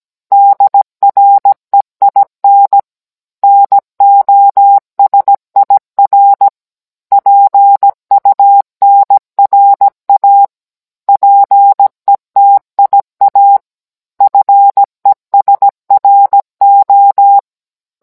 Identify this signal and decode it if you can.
Morse code, heard as a keyed tone, about 17 wpm, 800 Hz. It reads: DREIN NOSIR PUNRA PETIA FESRO